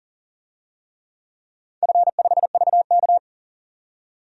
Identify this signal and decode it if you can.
Morse code, heard as a keyed tone, about 40 wpm, 690 Hz. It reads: F5VK
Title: DXCC-100 Call Signs - 4 Characters - 40wpm